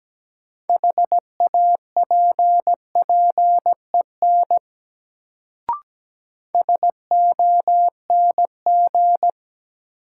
Morse code, heard as a keyed tone, about 17 wpm, 700 Hz. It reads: HAPPEN SONG
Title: Sentences from Top 400 Words - New Word Drill - 17wpm